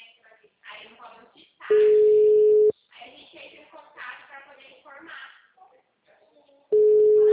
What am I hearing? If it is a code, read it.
Ringback tone (Europe, 425 Hz), with speech.